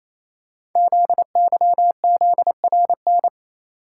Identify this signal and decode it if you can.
Morse code, heard as a keyed tone, about 27 wpm, 700 Hz. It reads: ZYZRN
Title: Code Groups - 5 Letters - 28wpm